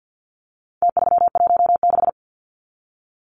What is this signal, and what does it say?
Morse code, heard as a keyed tone, about 49 wpm, 700 Hz. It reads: T316
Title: Numbers - Contest Style - 4-Digit 50wpm